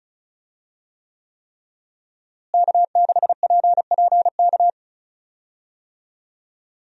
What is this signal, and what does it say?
Morse code, heard as a keyed tone, about 36 wpm, 700 Hz. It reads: K6PPK
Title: USA Call Signs - 5 Characters - Repeated 3x - 35wpm